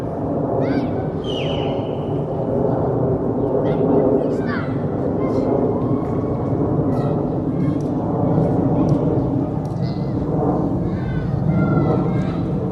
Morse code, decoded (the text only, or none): none